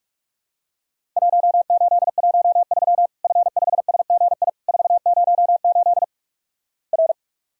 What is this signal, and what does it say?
Morse code, heard as a keyed tone, about 48 wpm, 700 Hz. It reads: 1813 F5SGI 408 R
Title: Worked All Europe DX Contest - QTC Exchange - 45wpm